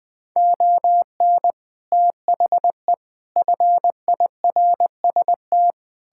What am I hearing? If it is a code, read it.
Morse code, heard as a keyed tone, about 21 wpm, 700 Hz. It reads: ON THE FIRST
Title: Sets of 6 Words - 20wpm